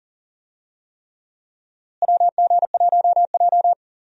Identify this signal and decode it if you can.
Morse code, heard as a keyed tone, about 40 wpm, 700 Hz. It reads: WG1J
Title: USA Call Signs - 4 Characters - Repeated 3x - 40wpm